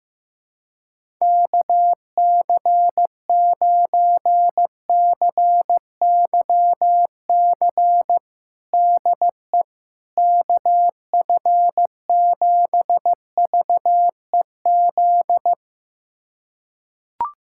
Morse code, heard as a keyed tone, about 15 wpm, 700 Hz. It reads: KC9CYC DE KF7VEZ
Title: USA Callsign DE Callsign 15wpm